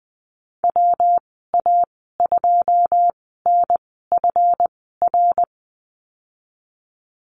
Morse code, heard as a keyed tone, about 20 wpm, spaced as for 15 wpm, 700 Hz. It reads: WA2NFR
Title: 173 Lesson 28g - Number 9 - Call Signs 20wpm